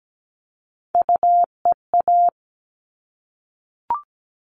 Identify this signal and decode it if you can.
Morse code, heard as a keyed tone, about 17 wpm, 700 Hz. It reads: UEA
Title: All Cut Numbers - 3-Digit 17wpm